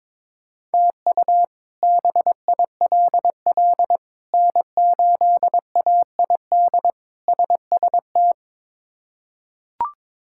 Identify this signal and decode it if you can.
Morse code, heard as a keyed tone, about 22 wpm, 700 Hz. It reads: TU BILL N8AID SST
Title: K1USN SST QSO Practice - 22wpm